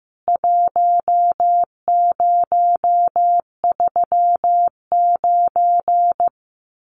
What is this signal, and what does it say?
Morse code, heard as a keyed tone, about 15 wpm, 700 Hz. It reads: 1039
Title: CWT Exchange 15wpm